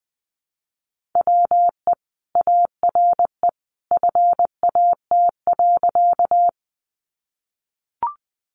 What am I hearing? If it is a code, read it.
Morse code, heard as a keyed tone, about 20 wpm, 700 Hz. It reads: WE ARE FAT.
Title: Sentences from Top 600 Words 20wpm